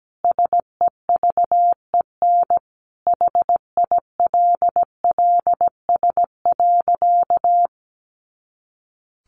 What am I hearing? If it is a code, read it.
Morse code, heard as a keyed tone, about 17 wpm, 700 Hz. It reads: SEVEN HILLS.
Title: A Sound Of Thunder 17wpm